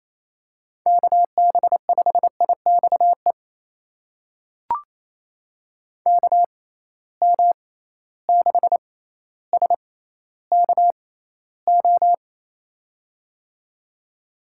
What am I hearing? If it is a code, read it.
Morse code, heard as a keyed tone, about 28 wpm, spaced as for 14 wpm, 700 Hz. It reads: KB5IXE KM6SKO
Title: USA Call Signs - 6 Characters - 6x Character Spacing - 28wpm